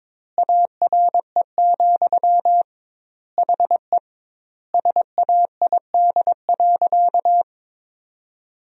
Morse code, heard as a keyed tone, about 22 wpm, 700 Hz. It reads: ARE, HE SAID.
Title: 2x Word-Spacing - Sentences from Top 100 Words - 22wpm